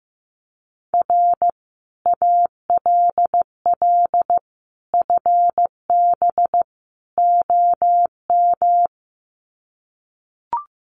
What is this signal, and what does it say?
Morse code, heard as a keyed tone, about 15 wpm, 700 Hz. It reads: R ALL FB OM
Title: Common QSO Phrases 15wpm